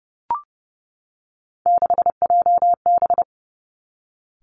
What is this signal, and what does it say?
Morse code, heard as a keyed tone, about 30 wpm, 700 Hz. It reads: 6JB
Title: Three Challenging Letter-Number Combinations 30wpm